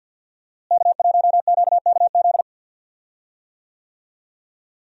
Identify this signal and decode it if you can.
Morse code, heard as a keyed tone, about 52 wpm, 700 Hz. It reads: K1QKZ
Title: USA Call Signs - 5 Characters - Repeated 2x - 50wpm